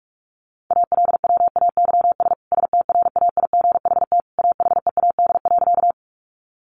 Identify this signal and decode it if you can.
Morse code, heard as a keyed tone, about 44 wpm, 700 Hz. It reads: ALWAYS STRAIGHT AHEAD.
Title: Sentences from Top 1100 Words 45wpm